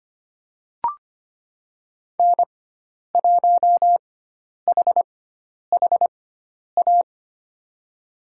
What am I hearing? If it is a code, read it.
Morse code, heard as a keyed tone, about 25 wpm, spaced as for 10 wpm, 700 Hz. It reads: N1HHA
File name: Lesson 16g - Letter c - Call Signs 25wpm